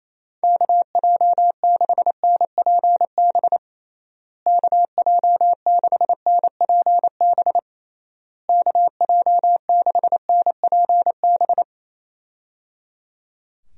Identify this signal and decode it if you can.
Morse code, heard as a keyed tone, about 28 wpm, 700 Hz. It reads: KJ6NPB KJ6NPB KJ6NPB
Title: USA Call Signs - 6 Characters - Repeated 3x - 28wpm